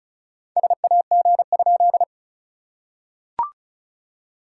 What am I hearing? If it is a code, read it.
Morse code, heard as a keyed tone, about 36 wpm, 690 Hz. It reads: SAG?
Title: SOTA QSO Practice 35wpm